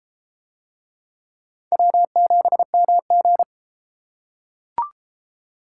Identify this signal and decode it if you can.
Morse code, heard as a keyed tone, about 32 wpm, 700 Hz. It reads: W7MG